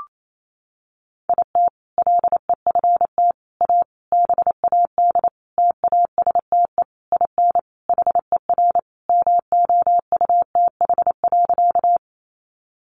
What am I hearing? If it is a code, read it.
Morse code, heard as a keyed tone, about 28 wpm, 700 Hz. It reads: IT LEFT A BAD TASTE IN HER MOUTH.
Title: Sentences from Top 1500 Words 28wpm